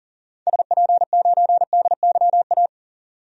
Morse code, heard as a keyed tone, about 40 wpm, 700 Hz. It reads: SP9DYA
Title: DXCC-100 Call Signs - 6 Characters - 40wpm